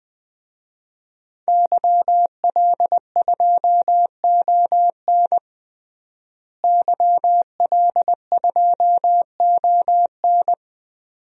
Morse code, heard as a keyed tone, about 20 wpm, 700 Hz. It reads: YL2ON YL2ON
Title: DXCC-100 Call Signs - 5 Characters - Repeated 2x - 20wpm